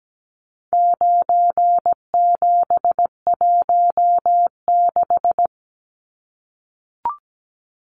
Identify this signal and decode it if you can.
Morse code, heard as a keyed tone, about 17 wpm, 700 Hz. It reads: 9716
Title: Numbers - 4-Digit 17wpm